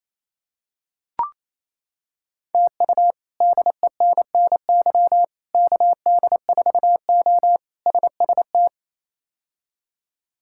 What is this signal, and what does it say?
Morse code, heard as a keyed tone, about 28 wpm, 700 Hz. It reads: TU DENNY KD4O SST
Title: K1USN SST QSO Practice - 28wpm